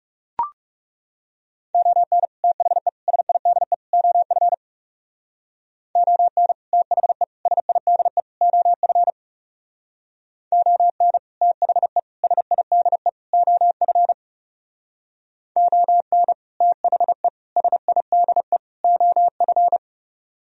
Morse code, heard as a keyed tone, about 38 wpm, 700 Hz. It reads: ON THE SIDE OF ON THE SIDE OF ON THE SIDE OF ON THE SIDE OF